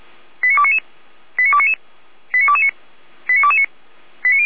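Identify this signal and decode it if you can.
Five-tone selective call (ZVEI), heard as quick runs of five tones, tones of about 70 ms. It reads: 89299 89299 89229 89229
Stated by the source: Received on SDR, ZVEI five-tone selective call 89174.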